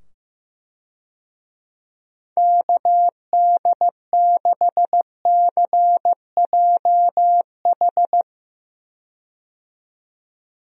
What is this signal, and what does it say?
Morse code, heard as a keyed tone, about 15 wpm, 700 Hz. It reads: KD6CJH
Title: USA Call Signs - 6 Characters - 15wpm